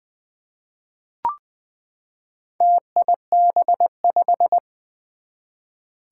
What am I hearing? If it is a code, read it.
Morse code, heard as a keyed tone, about 20 wpm, 700 Hz. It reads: TIB5
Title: Four Challenging Letter-Number Combinations 20wpm